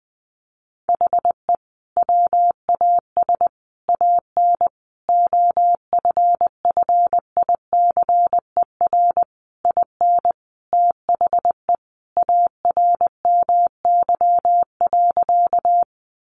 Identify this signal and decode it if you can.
Morse code, heard as a keyed tone, about 20 wpm, 700 Hz. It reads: HE WAS AN OFFICER IN THE ARMY.
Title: Sentences from Top 1200 Words 20wpm